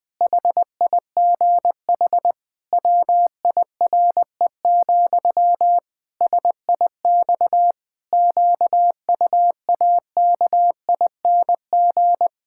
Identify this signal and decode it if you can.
Morse code, heard as a keyed tone, about 20 wpm, 700 Hz. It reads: HIGH WIRE, SIX QUAKING
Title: Pangram Challenge v2 - 20wpm